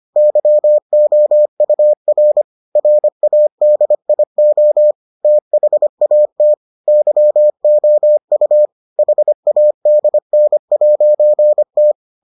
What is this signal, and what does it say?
Morse code, heard as a keyed tone, about 25 wpm, 600 Hz. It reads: YOUR RADIO THAT YOU HADN'T